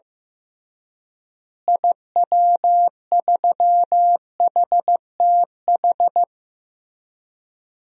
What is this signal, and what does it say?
Morse code, heard as a keyed tone, about 16 wpm, 700 Hz. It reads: IW3HTH
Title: DXCC-100 Call Signs - 6 Characters - Repeated 3x - 15wpm